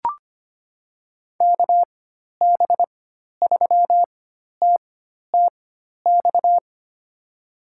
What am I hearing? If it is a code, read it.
Morse code, heard as a keyed tone, about 25 wpm, spaced as for 12 wpm, 700 Hz. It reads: KB3TTX
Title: USA Call Signs - 6 Characters - 4x Character Spacing - 25wpm